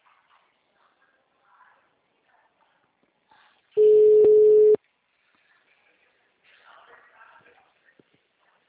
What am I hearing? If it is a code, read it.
Ringback tone (Europe, 425 Hz).
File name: out- -2008- - - .2471